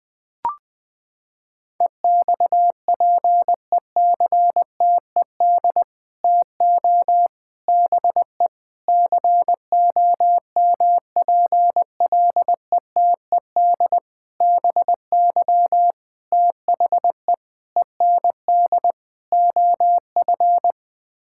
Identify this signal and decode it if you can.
Morse code, heard as a keyed tone, about 20 wpm, 700 Hz. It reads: EXPECTED TO BE COMPLETED BY THE END OF
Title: Sets of 8 Words - 20wpm